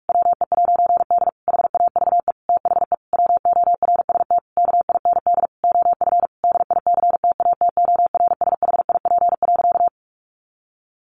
Morse code, heard as a keyed tone, about 44 wpm, 700 Hz. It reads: WE'D HAVE THE WORST KIND OF DICTATORSHIP.
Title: A Sound Of Thunder 45wpm